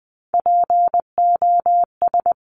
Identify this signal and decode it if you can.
Morse code, heard as a keyed tone, about 20 wpm, 700 Hz. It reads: POS
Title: Top Three Letter Combinations in Words 20wpm